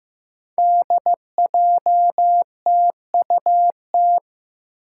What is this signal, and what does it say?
Morse code, heard as a keyed tone, about 15 wpm, 700 Hz. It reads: DJTUT
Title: Five Challenging Letter-Number Combinations 15wpm